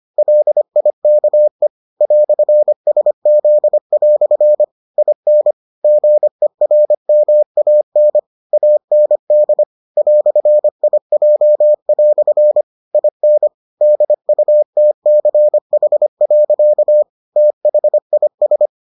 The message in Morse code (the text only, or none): LIKE "SZ" IN GERMAN AND "IJ" IN DUTCH. THIS